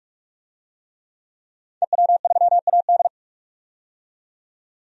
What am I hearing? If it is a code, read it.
Morse code, heard as a keyed tone, about 46 wpm, 700 Hz. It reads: EW3AD
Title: DXCC-100 Call Signs - 5 Characters - 45wpm